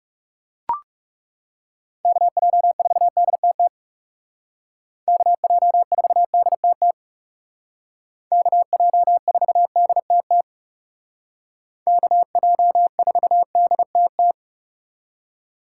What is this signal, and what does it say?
Morse code, heard as a keyed tone, about 37 wpm, 700 Hz. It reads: KJ4DTT KJ4DTT KJ4DTT KJ4DTT
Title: Speed Racing - USA Call Signs 30wpm